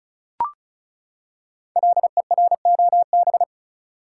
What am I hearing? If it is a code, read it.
Morse code, heard as a keyed tone, about 36 wpm, 700 Hz. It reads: LEROB
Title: Code Groups - 5 Letters - 35wpm